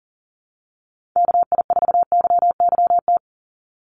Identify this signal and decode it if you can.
Morse code, heard as a keyed tone, about 42 wpm, 700 Hz. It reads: KI4YYT
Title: USA Call Signs 40wpm